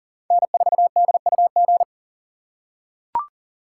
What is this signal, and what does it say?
Morse code, heard as a keyed tone, about 40 wpm, 700 Hz. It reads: N4DUG
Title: USA Call Signs - 2x Character Spacing - 40wpm